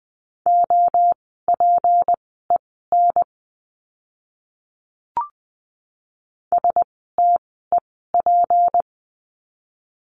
Morse code, heard as a keyed tone, about 20 wpm, spaced as for 15 wpm, 700 Hz. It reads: OPEN STEP
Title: 153 Lesson 25f - Letter p - Words 20wpm